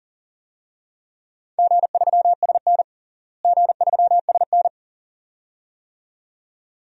Morse code, heard as a keyed tone, about 40 wpm, 700 Hz. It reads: G3SN G3SN
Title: DXCC-100 Call Signs - 4 Characters - Repeated 2x - 40wpm